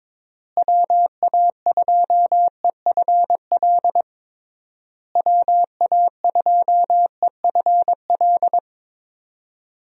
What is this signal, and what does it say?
Morse code, heard as a keyed tone, about 22 wpm, 700 Hz. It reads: WA2EFL WA2EFL
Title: USA Call Signs - 6 Characters - Repeated 3x - 22wpm